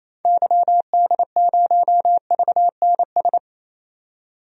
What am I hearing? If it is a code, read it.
Morse code, heard as a keyed tone, about 27 wpm, 700 Hz. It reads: YD0VNS
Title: DXCC-100 Call Signs - 6 Characters - 28wpm